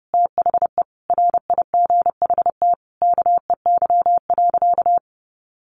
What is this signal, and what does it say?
Morse code, heard as a keyed tone, about 28 wpm, 700 Hz. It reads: THE RIGHT KEY.
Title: Sentences from Top 600 Words 30wpm